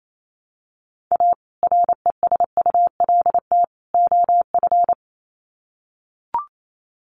Morse code, heard as a keyed tone, about 28 wpm, 700 Hz. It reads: A RESULT OF
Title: Sets of 3 Words 28wpm